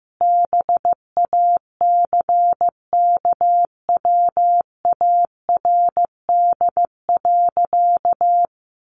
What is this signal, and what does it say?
Morse code, heard as a keyed tone, about 15 wpm, 700 Hz. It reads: BACKWARD.
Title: A Sound Of Thunder 15wpm